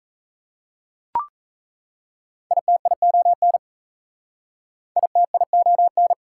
Morse code, heard as a keyed tone, about 36 wpm, 700 Hz. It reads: ITION ITION